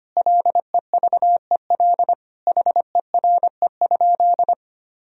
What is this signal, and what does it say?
Morse code, heard as a keyed tone, about 25 wpm, 700 Hz. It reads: LEVEL HERE?
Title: Sentences from Top 1000 Words - Review Part 4 - 25wpm